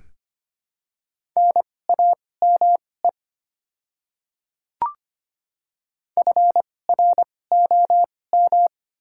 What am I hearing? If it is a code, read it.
Morse code, heard as a keyed tone, about 25 wpm, spaced as for 18 wpm, 700 Hz. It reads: NAME FROM
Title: 2x Character Spacing - Top 200 Words - 25wpm